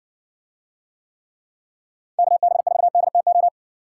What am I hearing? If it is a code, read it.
Morse code, heard as a keyed tone, about 60 wpm, 700 Hz. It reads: KB4DTO